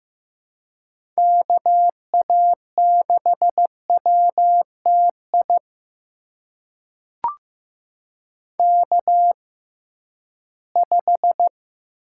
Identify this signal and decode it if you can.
Morse code, heard as a keyed tone, about 15 wpm, spaced as for 11 wpm, 700 Hz. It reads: KA6WTI K5
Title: USA Call Signs - 6x Character Spacing - 15wpm